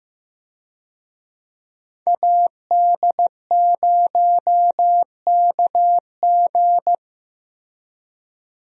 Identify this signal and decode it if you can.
Morse code, heard as a keyed tone, about 15 wpm, 700 Hz. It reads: AD0KG
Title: USA Call Signs - 5 Characters - Repeated 3x - 15wpm